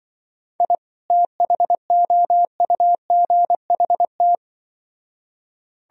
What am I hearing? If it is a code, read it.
Morse code, heard as a keyed tone, about 24 wpm, 700 Hz. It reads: I THOUGHT
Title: Speed-Racing - Sets of 2 Words 20wpm